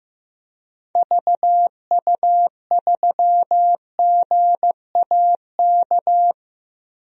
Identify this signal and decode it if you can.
Morse code, heard as a keyed tone, about 15 wpm, 700 Hz. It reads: VU3GAK
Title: DXCC-100 Call Signs - 6 Characters - Repeated 2x - 15wpm